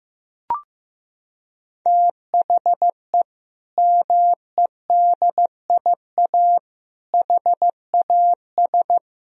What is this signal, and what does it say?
Morse code, heard as a keyed tone, about 15 wpm, 700 Hz. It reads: THE MEDIA HAS
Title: Sentences from Top 1000 Words - Review Part 4 - 15wpm